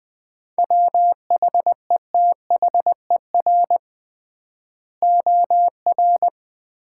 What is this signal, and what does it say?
Morse code, heard as a keyed tone, about 20 wpm, 700 Hz. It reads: WHETHER OR
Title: 3x Word-Spacing - Sets of 3 Words - 20wpm